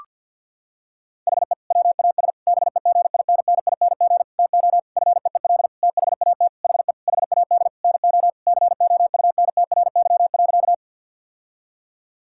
Morse code, heard as a keyed tone, about 51 wpm, 690 Hz. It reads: HE WAS BEGINNING TO FEEL THAT HE HAD NO COUNTRY.
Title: Sentences from Top 1000 Words - Review Part 5 - 50wpm